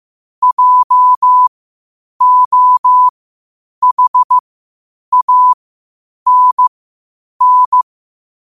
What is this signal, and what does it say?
Morse code, heard as a keyed tone, about 15 wpm, spaced as for 8 wpm, 1000 Hz. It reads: JOHANN
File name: qotdfiles / 8wpm